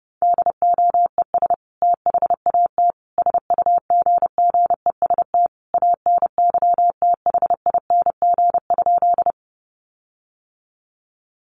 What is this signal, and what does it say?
Morse code, heard as a keyed tone, about 30 wpm, 700 Hz. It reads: DOES THAT SUGGEST ANYTHING?
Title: Sentences from Top 1300 Words 30wpm